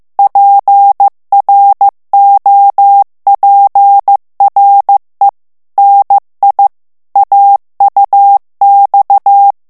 Morse code, heard as a keyed tone, about 15 wpm, 800 Hz. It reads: PROPRE NI AUX
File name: FRANCAIS TEXTE CLAIR 2 VITESSE 14 MOTS MINUTES